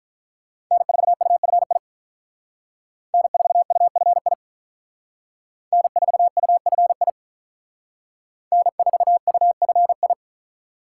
Morse code, heard as a keyed tone, about 45 wpm, 700 Hz. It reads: N4UFI N4UFI N4UFI N4UFI